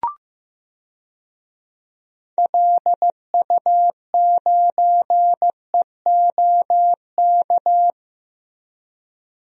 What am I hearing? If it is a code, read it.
Morse code, heard as a keyed tone, about 15 wpm, 700 Hz. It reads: LU9EOK
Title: DXCC-100 Call Signs - 6 Characters - Repeated 2x - 15wpm